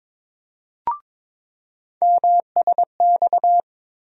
Morse code, heard as a keyed tone, about 22 wpm, 700 Hz. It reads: MSX
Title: USA Call Sign Postfixes - 3 Letters 22wpm